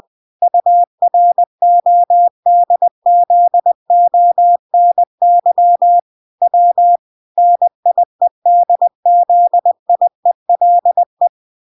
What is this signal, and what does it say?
Morse code, heard as a keyed tone, about 20 wpm, 700 Hz. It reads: URODZONY W NIEDZIELE